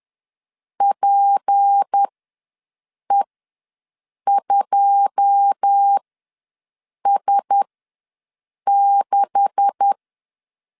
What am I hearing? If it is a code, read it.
Morse code, heard as a keyed tone, about 10 wpm, spaced as for 6 wpm, 800 Hz. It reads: PE2S6
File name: LEZIONI / 10 SIGNS / 10A SLASH ws 10wpm 4121